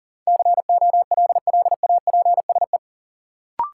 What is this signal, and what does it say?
Morse code, heard as a keyed tone, about 39 wpm, 700 Hz. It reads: COLLAPSE
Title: Sets of 5 Words Encore - New Word Drill - 40wpm